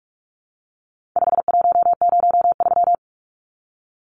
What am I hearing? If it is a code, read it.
Morse code, heard as a keyed tone, about 44 wpm, 700 Hz. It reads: 5103
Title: Numbers - 4-Digit 45wpm